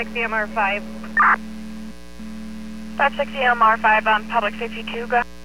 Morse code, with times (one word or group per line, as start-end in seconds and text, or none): none